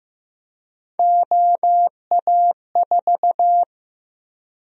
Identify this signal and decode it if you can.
Morse code, heard as a keyed tone, about 16 wpm, 700 Hz. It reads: OA4
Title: USA Call Sign Prefixes - 15wpm